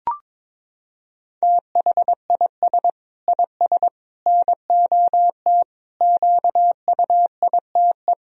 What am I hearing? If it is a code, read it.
Morse code, heard as a keyed tone, about 22 wpm, 700 Hz. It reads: THIS IS NOT QUITE
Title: Sentences from Top 1000 Words - Review Part 2 - 22wpm